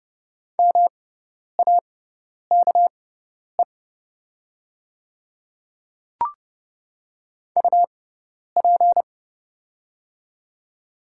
Morse code, heard as a keyed tone, about 30 wpm, spaced as for 10 wpm, 690 Hz. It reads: MAKE UP
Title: 6x Character Spacing - Top 500 Words - 30wpm